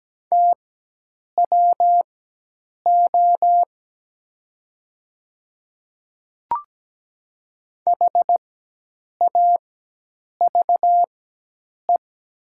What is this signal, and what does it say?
Morse code, heard as a keyed tone, about 17 wpm, spaced as for 8 wpm, 700 Hz. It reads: TWO HAVE
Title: 4x Character Spacing - Top 100 Words - 17wpm